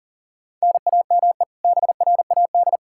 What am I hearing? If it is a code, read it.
Morse code, heard as a keyed tone, about 40 wpm, 700 Hz. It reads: NAME BRAD
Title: Practice QSO - Exchange 1 - Beginning - 40wpm